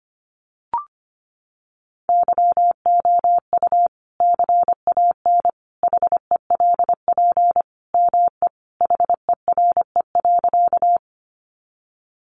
Morse code, heard as a keyed tone, about 25 wpm, 700 Hz. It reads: YOU CAN HELP ME HERE.
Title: Sentences from Top 200 Words 25wpm